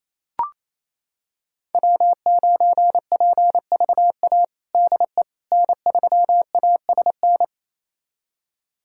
Morse code, heard as a keyed tone, about 28 wpm, 700 Hz. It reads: W9PVA DE N3ASN
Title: USA Callsign DE Callsign - 5 Characters - 28wpm